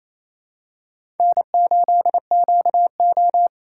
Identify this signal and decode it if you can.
Morse code, heard as a keyed tone, about 28 wpm, 700 Hz. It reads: N8QO
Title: USA Call Signs - 4x Character Spacing - 28wpm